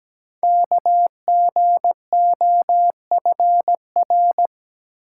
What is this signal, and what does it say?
Morse code, heard as a keyed tone, about 17 wpm, 700 Hz. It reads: KGOFR
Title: Code Groups - 5 Letters - 17wpm